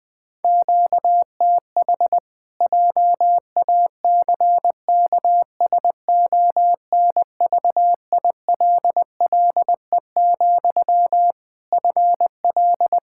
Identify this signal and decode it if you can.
Morse code, heard as a keyed tone, about 20 wpm, 700 Hz. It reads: QTH JACKSONVILLE, FL
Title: Practice QSO - Exchange 1 - Beginning - 20wpm